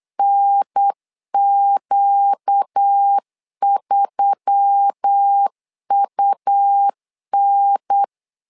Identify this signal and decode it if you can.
Morse code, heard as a keyed tone, about 8 wpm, 800 Hz. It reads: NQ3UN
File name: LEZIONI / 5 LBVC / 5 LBV 8wpm 2837